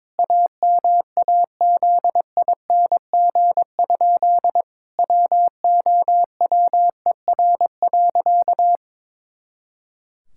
Morse code, heard as a keyed tone, about 22 wpm, 700 Hz. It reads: AMAZING? WOWER.
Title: Pangram Challenge v2 - 22wpm